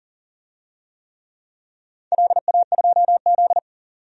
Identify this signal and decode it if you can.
Morse code, heard as a keyed tone, about 40 wpm, 700 Hz. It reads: LA2Z
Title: DXCC-100 Call Signs - 4 Characters - 40wpm